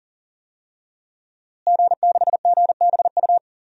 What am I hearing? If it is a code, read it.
Morse code, heard as a keyed tone, about 40 wpm, 700 Hz. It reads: G6GBU
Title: DXCC-100 Call Signs - 5 Characters - Repeated 3x - 40wpm